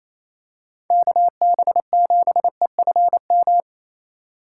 Morse code, heard as a keyed tone, about 28 wpm, 700 Hz. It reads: KB7EFM
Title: USA Call Signs - 6 Characters - 2x Character Spacing - 28wpm